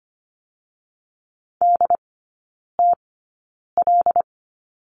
Morse code, heard as a keyed tone, about 25 wpm, spaced as for 9 wpm, 700 Hz. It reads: DTL
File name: Lesson 14e - Number 2 - Three Characters 25wpm